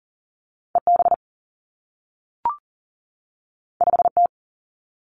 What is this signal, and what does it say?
Morse code, heard as a keyed tone, about 38 wpm, 700 Hz. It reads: E B 5 T
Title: Two Challenging Letter-Number Combinations 40wpm